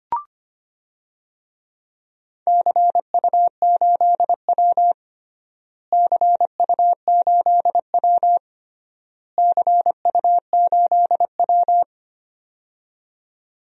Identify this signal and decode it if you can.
Morse code, heard as a keyed tone, about 25 wpm, 700 Hz. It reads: CU8W CU8W CU8W
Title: DXCC-100 Call Signs - 4 Characters - Repeated 3x - 25wpm